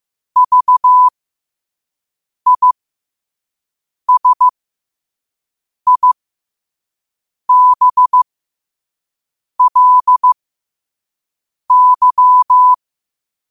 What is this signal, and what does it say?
Morse code, heard as a keyed tone, about 15 wpm, spaced as for 5 wpm, 1000 Hz. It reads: VISIBLY
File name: qotdfiles / 5wpm